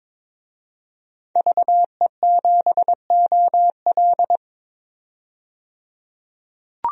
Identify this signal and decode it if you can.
Morse code, heard as a keyed tone, about 21 wpm, 700 Hz. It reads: VE7OL
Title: DXCC-100 Call Signs - 5 Characters - 22wpm